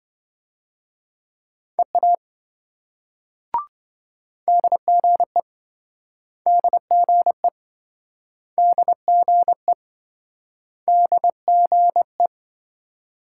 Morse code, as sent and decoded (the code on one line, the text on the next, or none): . .- / -.. --. / -.. --. / -.. --. / -.. --. .
EA DGE DGE DGE DGE